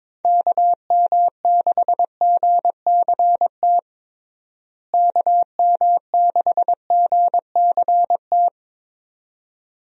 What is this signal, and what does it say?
Morse code, heard as a keyed tone, about 22 wpm, 700 Hz. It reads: KM6GCT KM6GCT
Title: USA Call Signs - 6 Characters - Repeated 2x - 22wpm